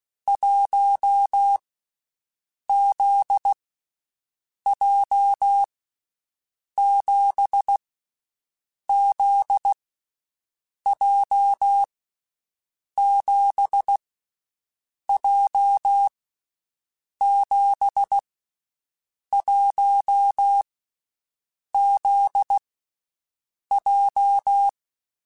Morse code, heard as a keyed tone, about 16 wpm, 800 Hz. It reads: 1 Z J 7 Z J 7 J 7 1 Z J
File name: Basic Sets / Basic Letter Groups Collective7 / JZ17@10